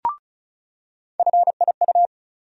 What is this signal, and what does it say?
Morse code, heard as a keyed tone, about 36 wpm, 700 Hz. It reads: FIU
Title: USA Call Sign Postfixes - 3 Letters 35wpm